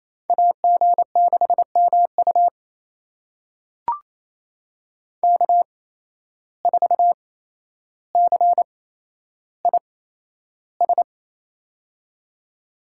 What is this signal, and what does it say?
Morse code, heard as a keyed tone, about 28 wpm, spaced as for 12 wpm, 700 Hz. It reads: AG6MU K4CIS
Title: USA Call Signs - 5 Characters - 8x Character Spacing - 28wpm